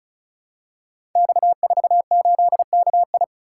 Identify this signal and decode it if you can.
Morse code, heard as a keyed tone, about 36 wpm, 700 Hz. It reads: X48KI
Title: Code Groups - 5 Letter:Numbers - 35wpm